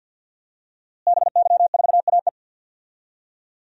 Morse code, heard as a keyed tone, about 52 wpm, 700 Hz. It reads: BY4AE
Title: DXCC-100 Call Signs - 5 Characters - Repeated 2x - 50wpm